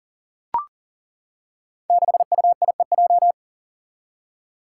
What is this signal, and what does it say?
Morse code, heard as a keyed tone, about 38 wpm, 700 Hz. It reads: 6UIEJ
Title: Five Challenging Letter-Number Combinations 40wpm